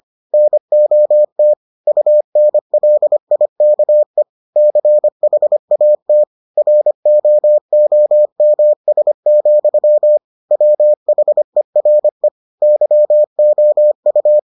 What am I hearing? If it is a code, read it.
Morse code, heard as a keyed tone, about 25 wpm, 600 Hz. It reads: NOT UNLIKE CHAT ROOMS, WHERE YOU